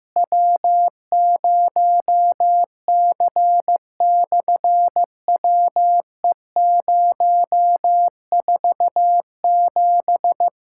Morse code, heard as a keyed tone, about 15 wpm, 700 Hz. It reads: W0C/WE047
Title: SOTA QSO Practice 15wpm